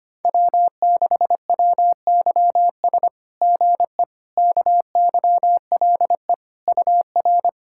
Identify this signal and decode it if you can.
Morse code, heard as a keyed tone, about 25 wpm, 700 Hz. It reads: W6WYS GE KYLE UR
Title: SOTA QSO Practice 25wpm